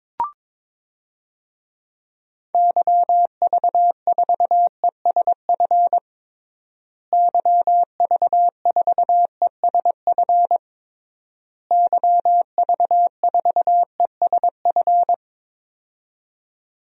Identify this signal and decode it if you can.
Morse code, heard as a keyed tone, about 22 wpm, 700 Hz. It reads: YV4ESF YV4ESF YV4ESF
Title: DXCC-100 Call Signs - 6 Characters - Repeated 3x - 22wpm